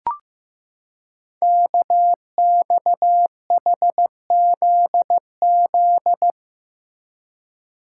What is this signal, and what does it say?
Morse code, heard as a keyed tone, about 15 wpm, 700 Hz. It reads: KXHZZ
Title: Code Groups - 5 Letters - 15wpm